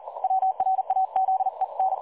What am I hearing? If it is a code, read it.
Morse code, heard as a keyed tone, about 39 wpm, 760 Hz. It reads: WRAZEN
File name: dlazaawansowanych / nagraniazpasma / websdr recording 3565.0kHz